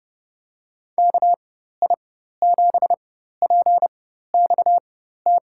Morse code, heard as a keyed tone, about 32 wpm, spaced as for 14 wpm, 700 Hz. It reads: KI7PXT
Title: USA Call Signs - 6 Characters - 4x Character Spacing - 30wpm